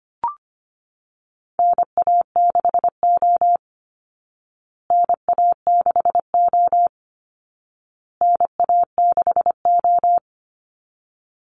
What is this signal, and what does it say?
Morse code, heard as a keyed tone, about 25 wpm, 700 Hz. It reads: NA6O NA6O NA6O
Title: Call Signs Repeated 2x - 25wpm